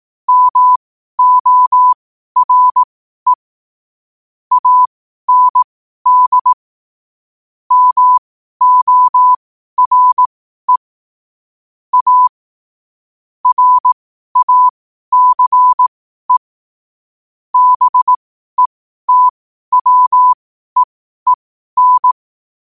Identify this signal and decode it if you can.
Morse code, heard as a keyed tone, about 18 wpm, spaced as for 12 wpm, 1000 Hz. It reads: MORE AND MORE A RACE BETWEEN
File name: qotdfiles / 12wpm